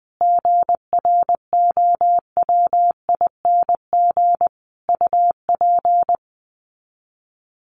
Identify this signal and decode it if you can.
Morse code, heard as a keyed tone, about 20 wpm, 700 Hz. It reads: GROWING UP
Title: Sets of 5 Words Encore 20wpm